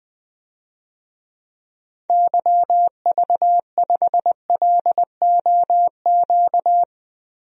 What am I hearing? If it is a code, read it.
Morse code, heard as a keyed tone, about 20 wpm, 700 Hz. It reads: YV5LOQ
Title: DXCC-100 Call Signs - 6 Characters - Repeated 2x - 20wpm